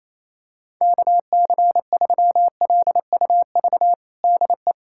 Morse code, heard as a keyed tone, about 27 wpm, 700 Hz. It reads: KC3LUV DE
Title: USA Callsign DE Callsign 28wpm